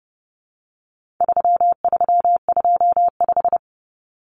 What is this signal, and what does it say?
Morse code, heard as a keyed tone, about 30 wpm, 700 Hz. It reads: 3325
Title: Common Cut Numbers - 4-Digit 30wpm